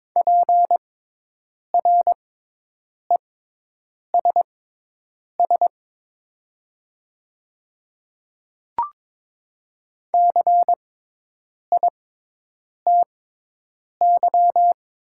Morse code, heard as a keyed tone, about 22 wpm, spaced as for 8 wpm, 700 Hz. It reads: PRESS CITY
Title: 6x Character Spacing - Top 300 Words - 22wpm